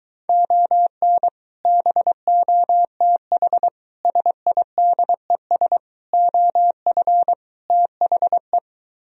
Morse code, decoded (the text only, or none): ON BOTH SIDES OF THE